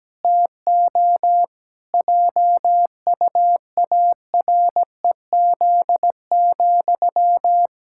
Morse code, heard as a keyed tone, about 17 wpm, 700 Hz. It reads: TO JUAREZ,
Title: Pangram Challenge v2 - 17wpm